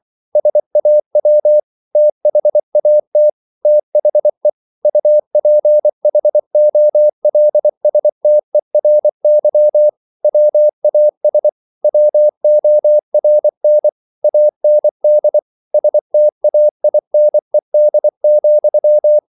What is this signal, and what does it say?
Morse code, heard as a keyed tone, about 24 wpm, 600 Hz. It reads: SAW THAT THE UPHOLSTERY WAS WORN AND STAINED,